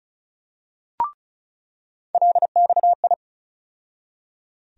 Morse code, heard as a keyed tone, about 34 wpm, 700 Hz. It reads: LXI
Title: USA Call Sign Postfixes - 3 Letters 35wpm